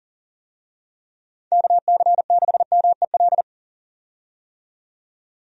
Morse code, heard as a keyed tone, about 40 wpm, 700 Hz. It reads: KC6MEL
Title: USA Call Signs - 6 Characters - 40wpm